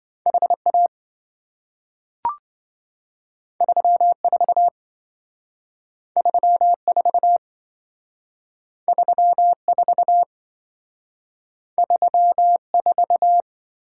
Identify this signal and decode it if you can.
Morse code, heard as a keyed tone, about 27 wpm, 700 Hz. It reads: HA 34 34 34 34